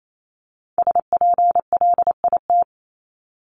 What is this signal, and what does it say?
Morse code, heard as a keyed tone, about 28 wpm, 700 Hz. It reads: SPLIT
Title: Top 1751-2000 Words 28wpm